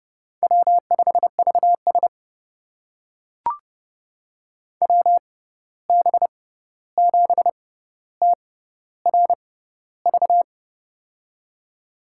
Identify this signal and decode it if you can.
Morse code, heard as a keyed tone, about 30 wpm, spaced as for 14 wpm, 700 Hz. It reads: W5VS WB7TRV
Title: USA Call Signs - 6x Character Spacing - 30wpm